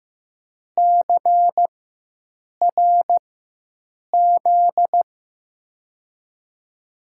Morse code, heard as a keyed tone, about 15 wpm, spaced as for 7 wpm, 700 Hz. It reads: CRZ
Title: USA Call Signs - 5 Characters - 4x Character Spacing - 15wpm